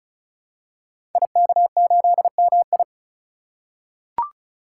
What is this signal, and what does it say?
Morse code, heard as a keyed tone, about 36 wpm, 700 Hz. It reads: IK8MI
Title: Code Groups - 5 Letter:Numbers - 35wpm